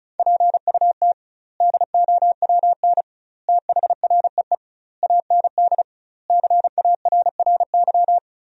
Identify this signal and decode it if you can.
Morse code, heard as a keyed tone, about 36 wpm, 700 Hz. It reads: PUT DOWN THREE AND CARRY
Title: 2x Word-Spacing - Sentences from Top 400 Words - 35wpm